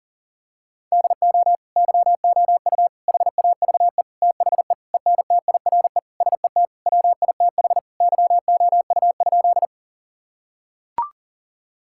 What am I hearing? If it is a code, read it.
Morse code, heard as a keyed tone, about 40 wpm, 700 Hz. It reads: DO YOU HAVE THE ENTIRE SET WITH YOU?
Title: Sentences from Top 1000 Words - Review Part 5 - 40wpm